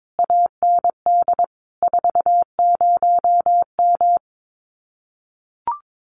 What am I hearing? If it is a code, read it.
Morse code, heard as a keyed tone, about 22 wpm, 700 Hz. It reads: AND 40M
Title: Common QSO Phrases Encore 22wpm